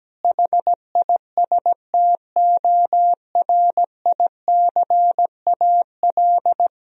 Morse code, heard as a keyed tone, about 17 wpm, 700 Hz. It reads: HISTORICAL
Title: Sentences from Top 1000 Words - Review Part 4 - 17wpm